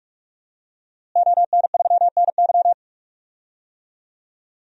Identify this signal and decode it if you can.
Morse code, heard as a keyed tone, about 43 wpm, 700 Hz. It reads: ON3NY
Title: DXCC-100 Call Signs - 5 Characters - Repeated 3x - 45wpm